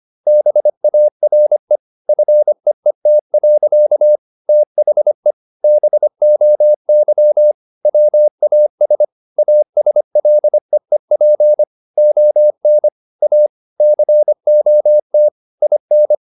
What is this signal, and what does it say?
Morse code, heard as a keyed tone, about 25 wpm, 600 Hz. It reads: BARE FEET. THE BOY WAS ASLEEP ON A COT IN
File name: the old man and the sea part 1 25wpm